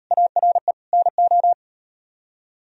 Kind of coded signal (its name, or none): Morse code